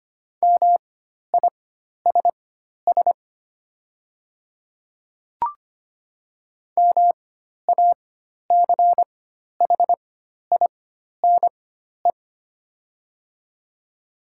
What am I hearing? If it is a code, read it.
Morse code, heard as a keyed tone, about 25 wpm, spaced as for 12 wpm, 700 Hz. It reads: MISS MACHINE
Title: 4x Character Spacing - Top 500 Words - 25wpm